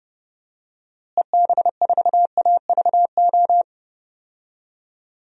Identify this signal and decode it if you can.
Morse code, heard as a keyed tone, about 30 wpm, 700 Hz. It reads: EB4AVO
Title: DXCC-100 Call Signs - 6 Characters - Repeated 3x - 30wpm